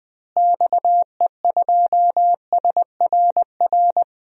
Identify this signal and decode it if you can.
Morse code, heard as a keyed tone, about 20 wpm, 700 Hz. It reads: XE2SRR
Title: DXCC-100 Call Signs - 6 Characters - Repeated 3x - 20wpm